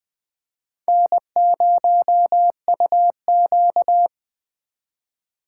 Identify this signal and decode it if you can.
Morse code, heard as a keyed tone, about 21 wpm, 700 Hz. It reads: N0UQ
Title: Speed Racing - USA 1x2 Call Signs 17wpm